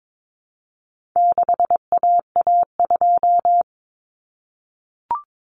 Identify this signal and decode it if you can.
Morse code, heard as a keyed tone, about 22 wpm, 700 Hz. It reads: 6AA2
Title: Common Cut Numbers - 4-Digit 22wpm